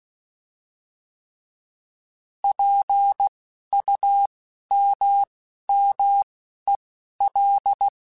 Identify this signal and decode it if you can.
Morse code, heard as a keyed tone, about 16 wpm, spaced as for 11 wpm, 800 Hz. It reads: PUMMEL
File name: Alphabetical / Alphabetical mp3s / Ps @10